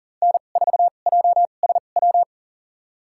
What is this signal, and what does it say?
Morse code, heard as a keyed tone, about 40 wpm, spaced as for 29 wpm, 700 Hz. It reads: N4JSW
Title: USA Call Signs - 2x Character Spacing - 40wpm